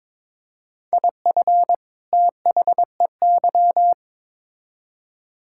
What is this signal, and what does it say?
Morse code, heard as a keyed tone, about 22 wpm, 700 Hz. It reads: IF THEY
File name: Build A Phrase - 2-Word Phrases - 22wpm